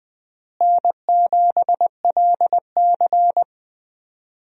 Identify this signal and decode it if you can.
Morse code, heard as a keyed tone, about 20 wpm, 700 Hz. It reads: N7LC
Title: USA Call Signs - 4 Characters - Repeated 2x - 20wpm